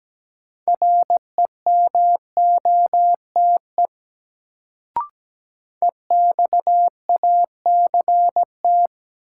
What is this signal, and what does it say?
Morse code, heard as a keyed tone, about 17 wpm, 700 Hz. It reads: REMOTE EXACT
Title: Sentences from Top 1300 Words - New Word Drill - 17wpm